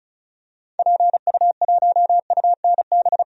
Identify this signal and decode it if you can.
Morse code, heard as a keyed tone, about 36 wpm, 700 Hz. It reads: PU1UNB
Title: DXCC-100 Call Signs - 6 Characters - Repeated 2x - 35wpm